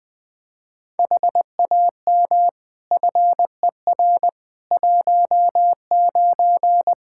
Morse code, heard as a keyed tone, about 20 wpm, 700 Hz. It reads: HAM FER 19
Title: Practice QSO - Exchange 3 - More Info - 20wpm